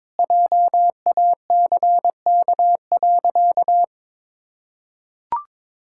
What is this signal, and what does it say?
Morse code, heard as a keyed tone, about 22 wpm, 700 Hz. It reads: JACK.
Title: Pangram Challenge v2 - 22wpm